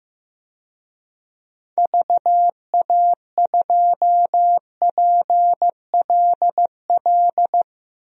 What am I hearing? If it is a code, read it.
Morse code, heard as a keyed tone, about 16 wpm, 700 Hz. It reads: VA2PLL
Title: DXCC-100 Call Signs - 6 Characters - 15wpm